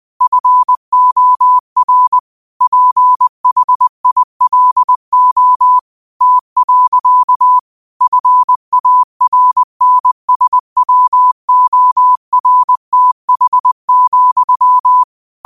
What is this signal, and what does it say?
Morse code, heard as a keyed tone, about 20 wpm, 1000 Hz. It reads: FOR PHILO T. FARNSWORTH,